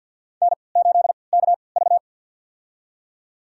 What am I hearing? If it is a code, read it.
Morse code, heard as a keyed tone, about 52 wpm, spaced as for 28 wpm, 700 Hz. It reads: N8KV
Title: USA Call Signs - 4 Characters - 3x Character Spacing - 50wpm